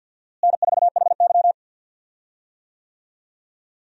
Morse code, heard as a keyed tone, about 52 wpm, 700 Hz. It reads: N4HY
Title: USA Call Signs - 4 Characters - Repeated 2x - 50wpm